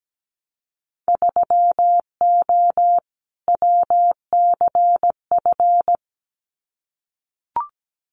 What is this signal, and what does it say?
Morse code, heard as a keyed tone, about 17 wpm, 700 Hz. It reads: 3O WCF
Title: Winter Field Day Exchanges 17wpm